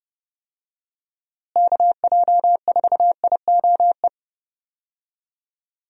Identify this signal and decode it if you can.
Morse code, heard as a keyed tone, about 30 wpm, 700 Hz. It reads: KJ4IOE
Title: USA Call Signs - 6 Characters - Repeated 2x - 30wpm